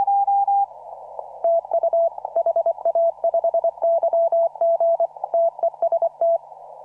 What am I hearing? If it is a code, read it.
Morse code, heard as a keyed tone, about 24 wpm, 670 Hz. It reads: TU HA5YG TEST
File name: CW / RHOB CW 2017 HA6P 0001